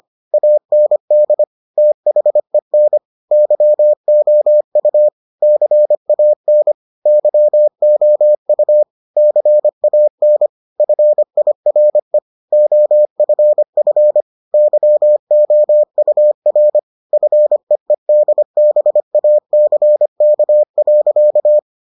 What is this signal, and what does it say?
Morse code, heard as a keyed tone, about 25 wpm, 600 Hz. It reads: AND THEN YOU CAN YOU CAN FIRE OFF YOUR FEEDBACK.